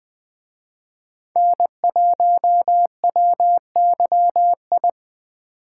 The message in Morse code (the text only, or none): N1WYI